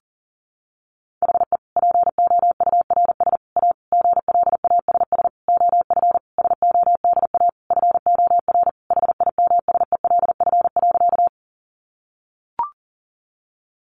Morse code, heard as a keyed tone, about 40 wpm, 700 Hz. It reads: HE POURS A GLASS OF SODA FOR HIMSELF.
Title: Additional Easy Phrases 40wpm